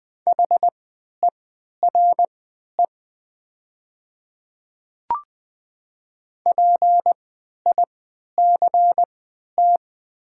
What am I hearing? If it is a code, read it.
Morse code, heard as a keyed tone, about 20 wpm, spaced as for 11 wpm, 700 Hz. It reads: HERE PICT
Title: 3x Character Spacing - Top 200 Words - 20wpm